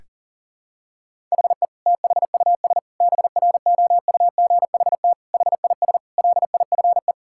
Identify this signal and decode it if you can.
Morse code, heard as a keyed tone, about 40 wpm, 700 Hz. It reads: HE THUS BROUGHT HIS LIFE
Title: Sentences from Top 1000 Words - Review Part 5 - 40wpm